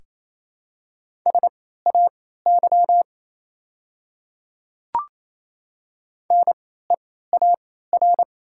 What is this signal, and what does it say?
Morse code, heard as a keyed tone, about 28 wpm, spaced as for 16 wpm, 700 Hz. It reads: SAY NEAR
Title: 3x Character Spacing - Top 300 Words - 28wpm